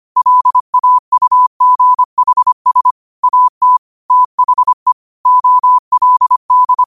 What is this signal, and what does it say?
Morse code, heard as a keyed tone, about 25 wpm, 1000 Hz. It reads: LAUGHS AT THE OLD
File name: qotdfiles / 25wpm